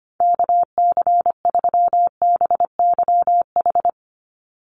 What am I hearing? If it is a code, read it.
Morse code, heard as a keyed tone, about 25 wpm, 700 Hz. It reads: KC3BYH
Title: USA Call Signs 25wpm